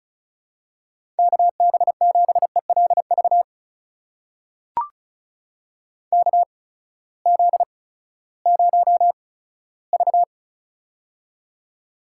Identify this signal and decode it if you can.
Morse code, heard as a keyed tone, about 36 wpm, spaced as for 18 wpm, 700 Hz. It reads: KB7ELV KZ0V
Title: USA Call Signs - 8x Character Spacing - 35wpm